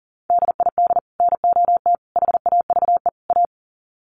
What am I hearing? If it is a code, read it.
Morse code, heard as a keyed tone, about 40 wpm, 700 Hz. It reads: DID NOT HAVE A
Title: Sets of 4 Words 40wpm